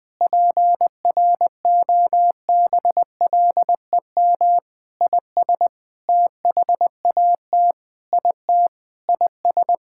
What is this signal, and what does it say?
Morse code, heard as a keyed tone, about 20 wpm, 700 Hz. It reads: PROBLEM IS THAT IT IS
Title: Sentences with 6-Word Phrases - 20wpm